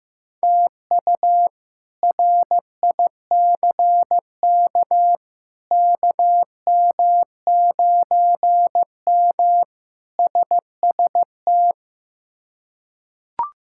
Morse code, heard as a keyed tone, about 15 wpm, 700 Hz. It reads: TU RICK KM9M SST
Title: K1USN SST QSO Practice - 15wpm